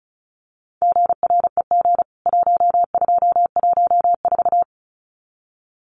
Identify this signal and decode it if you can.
Morse code, heard as a keyed tone, about 34 wpm, 700 Hz. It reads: GREG 1214
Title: CWT Exchange 35wpm